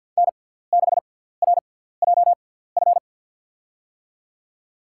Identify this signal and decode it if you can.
Morse code, heard as a keyed tone, about 50 wpm, spaced as for 17 wpm, 700 Hz. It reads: N6RJF
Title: USA Call Signs - 5 Characters - 6x Character Spacing - 50wpm